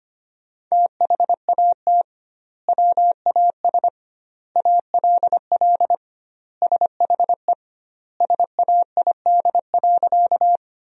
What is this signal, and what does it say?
Morse code, heard as a keyed tone, about 25 wpm, 700 Hz. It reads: THAT WAS ALL SHE SAID.
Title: 2x Word-Spacing - Sentences from Top 100 Words - 25wpm